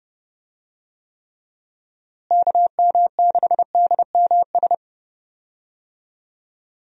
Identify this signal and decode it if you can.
Morse code, heard as a keyed tone, about 32 wpm, 700 Hz. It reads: KM6DMS
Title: USA Call Signs - 6 Characters - Repeated 3x - 30wpm